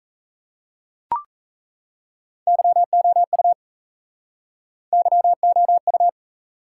Morse code, heard as a keyed tone, about 35 wpm, 700 Hz. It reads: YOU YOU